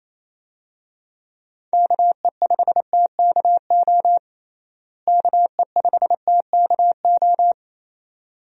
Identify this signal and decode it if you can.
Morse code, heard as a keyed tone, about 28 wpm, 700 Hz. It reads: KE5TKO KE5TKO
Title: USA Call Signs - 6 Characters - Repeated 2x - 28wpm